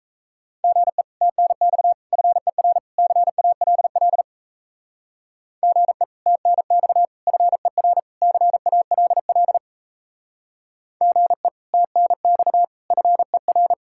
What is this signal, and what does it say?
Morse code, heard as a keyed tone, about 34 wpm, 700 Hz. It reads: GE TNX FER CALL GE TNX FER CALL GE TNX FER
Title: Speed-Racing Common QSO Phrases Encore - 28wpm